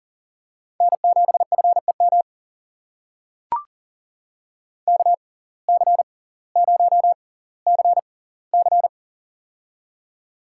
Morse code, heard as a keyed tone, about 39 wpm, spaced as for 20 wpm, 700 Hz. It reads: N7FEM KC0CC
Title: USA Call Signs - 5 Characters - 6x Character Spacing - 40wpm